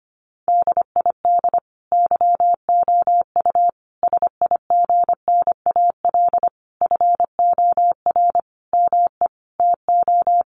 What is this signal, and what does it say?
Morse code, heard as a keyed tone, about 25 wpm, 700 Hz. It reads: DID YOU SIGNAL FOR ME TO
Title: Sentences from Top 1400 Words 25wpm